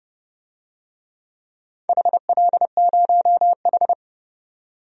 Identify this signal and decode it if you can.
Morse code, heard as a keyed tone, about 30 wpm, 700 Hz. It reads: HL0H
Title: DXCC-100 Call Signs - 4 Characters - 30wpm